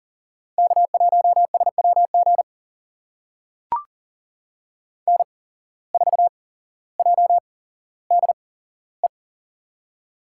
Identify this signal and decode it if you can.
Morse code, heard as a keyed tone, about 41 wpm, spaced as for 17 wpm, 700 Hz. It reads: K1SWG N4JDE
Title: USA Call Signs - 5 Characters - 8x Character Spacing - 40wpm